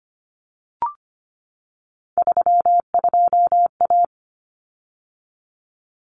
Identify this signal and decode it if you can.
Morse code, heard as a keyed tone, about 25 wpm, 700 Hz. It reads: 32A
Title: Common Cut Numbers - 3-Digit 25wpm